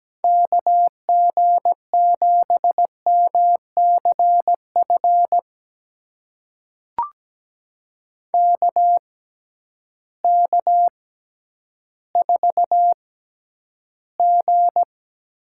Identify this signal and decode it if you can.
Morse code, heard as a keyed tone, about 17 wpm, spaced as for 10 wpm, 700 Hz. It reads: KG7MCF KK4G
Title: USA Call Signs - 6 Characters - 6x Character Spacing - 17wpm